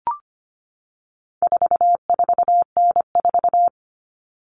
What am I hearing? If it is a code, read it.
Morse code, heard as a keyed tone, about 25 wpm, 700 Hz. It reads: 44N4
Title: All Cut Numbers - 4-Digit 25wpm